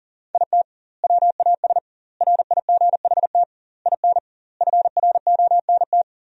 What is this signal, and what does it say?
Morse code, heard as a keyed tone, about 37 wpm, 700 Hz. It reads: IT WAS RIGHT IN FRONT
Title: 2x Word-Spacing - Sentences from Top 500 Words - 40wpm